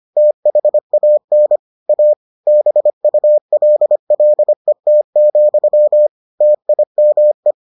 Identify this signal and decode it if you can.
Morse code, heard as a keyed tone, about 25 wpm, 600 Hz. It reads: THAN A BULLET, TIME